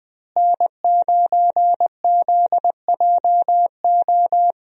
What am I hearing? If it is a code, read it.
Morse code, heard as a keyed tone, about 20 wpm, 700 Hz. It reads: N9ZJO
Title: USA Call Signs - 5 Characters - Repeated 3x - 20wpm